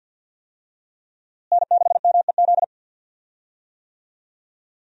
Morse code, heard as a keyed tone, about 47 wpm, 700 Hz. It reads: N6MEZ
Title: USA Call Signs - 5 Characters - Repeated 2x - 50wpm